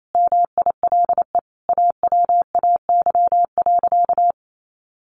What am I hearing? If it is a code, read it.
Morse code, heard as a keyed tone, about 28 wpm, 700 Hz. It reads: MILE AWAY.
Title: Sentences from Top 1300 Words 28wpm